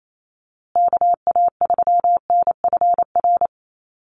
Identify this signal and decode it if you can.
Morse code, heard as a keyed tone, about 28 wpm, 700 Hz. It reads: KA3NFR
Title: USA Call Signs Repeated 3x 28wpm